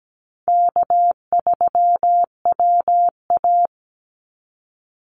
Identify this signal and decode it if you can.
Morse code, heard as a keyed tone, about 17 wpm, 700 Hz. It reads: K3WA
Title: Easy Call Signs Repeated 3x - 17wpm